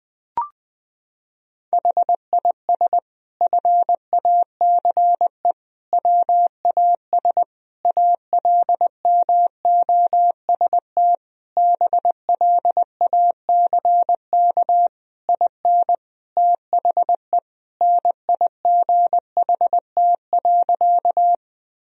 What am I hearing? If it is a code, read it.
Morse code, heard as a keyed tone, about 20 wpm, 700 Hz. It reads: HIS FACE WAS ALMOST BLACK IN THE NIGHT.
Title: Sentences from Top 1000 Words - Review Part 4 - 20wpm